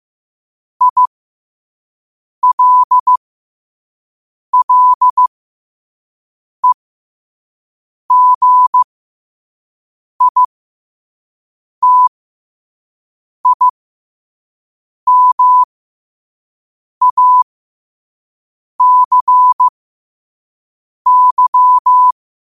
Morse code, heard as a keyed tone, about 15 wpm, spaced as for 5 wpm, 1000 Hz. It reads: ILLEGITIMACY
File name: qotdfiles / 5wpm